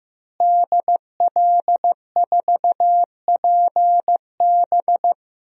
Morse code, heard as a keyed tone, about 15 wpm, 700 Hz. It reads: DL4PB
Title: DXCC-100 Call Signs - 5 Characters - Repeated 2x - 15wpm